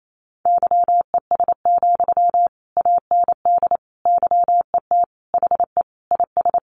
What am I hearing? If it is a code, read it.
Morse code, heard as a keyed tone, about 28 wpm, 700 Hz. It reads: YES, AND YET HE IS
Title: Sentences from Top 700 Words 28wpm